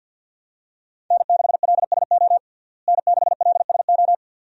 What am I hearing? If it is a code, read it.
Morse code, heard as a keyed tone, about 52 wpm, 700 Hz. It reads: N6LSO N6LSO
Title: USA Call Signs - 5 Characters - Repeated 2x - 50wpm